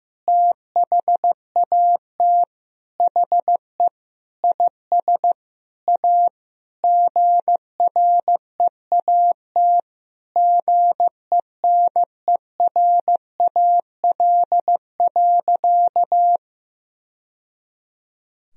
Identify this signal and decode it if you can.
Morse code, heard as a keyed tone, about 15 wpm, 700 Hz. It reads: THAT HE IS A GREAT GENERAL.
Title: Sentences from Top 1000 Words - Review Part 3 - 15wpm